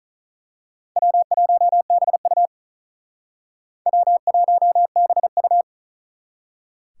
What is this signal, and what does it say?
Morse code, heard as a keyed tone, about 39 wpm, 700 Hz. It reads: W1BU W1BU
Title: Speed Racing - USA 1x2 Call Signs 35wpm